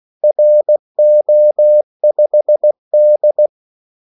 Morse code, heard as a keyed tone, about 16 wpm, 600 Hz. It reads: RO5D
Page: Kaldesignaler 16 wpm